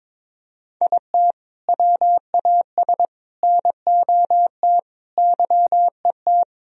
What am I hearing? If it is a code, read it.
Morse code, heard as a keyed tone, about 22 wpm, 700 Hz. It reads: IT WAS NOT YET
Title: Sentences from Top 1000 Words - Review Part 1 - 22wpm